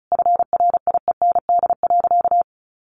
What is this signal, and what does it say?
Morse code, heard as a keyed tone, about 34 wpm, 700 Hz. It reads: FRIEND.
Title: Sentences from Top 1100 Words 35wpm